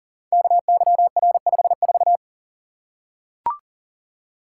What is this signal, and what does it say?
Morse code, heard as a keyed tone, about 42 wpm, 700 Hz. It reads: KYR54
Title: Code Groups - 5 Letter:Numbers - 40wpm